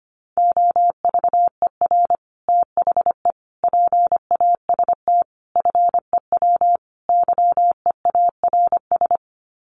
Morse code, heard as a keyed tone, about 25 wpm, 700 Hz. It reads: OVER THE PAST FEW YEARS
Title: Sets of 5 Words Encore 25wpm